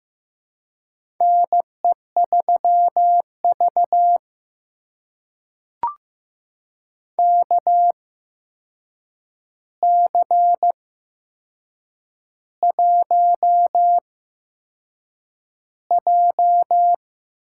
Morse code, heard as a keyed tone, about 15 wpm, spaced as for 6 wpm, 700 Hz. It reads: NE3V KC1J
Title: USA Call Signs - 4 Characters - 8x Character Spacing - 15wpm